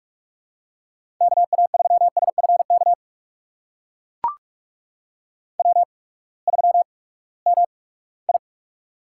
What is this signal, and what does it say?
Morse code, heard as a keyed tone, about 44 wpm, spaced as for 13 wpm, 700 Hz. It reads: KA3SF KW3MI
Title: USA Call Signs - 8x Character Spacing - 45wpm